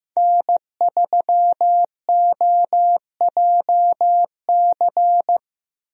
Morse code, heard as a keyed tone, about 15 wpm, 700 Hz. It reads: N3OJC
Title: USA Call Signs - 5 Characters - Repeated 2x - 15wpm